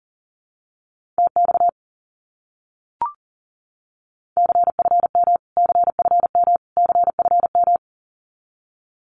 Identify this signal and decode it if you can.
Morse code, heard as a keyed tone, about 40 wpm, 700 Hz. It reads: TX CFM CFM CFM
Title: Abbreviations Repeated 3x - 40wpm